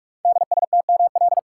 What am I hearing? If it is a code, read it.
Morse code, heard as a keyed tone, about 44 wpm, 690 Hz. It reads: DSTML
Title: Code Groups - 5 Letters - 45wpm